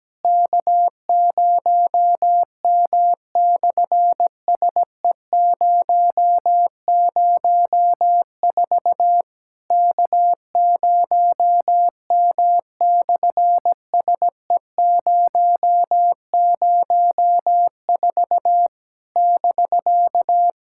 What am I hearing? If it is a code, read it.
Morse code, heard as a keyed tone, about 17 wpm, 700 Hz. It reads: K0M/SE004 K0M/SE004 <BK>
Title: SOTA QSO Practice 17wpm